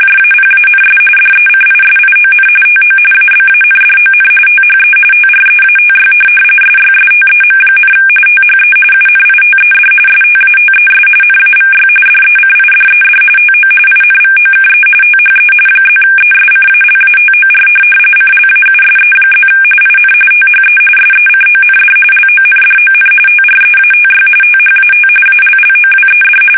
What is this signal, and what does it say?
RTTY, about 45.45 baud. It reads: JAPANS DECISION TO ABANDON THE TREATY SYSTEM, THE UNITED STATES NAVY DECIDED TO INVOKE THE +ESCALATOR CLAUSE+ IN THE SECOND LONDON TREATY THAT ALLOWED
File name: fsk850 / USS-Massachusetts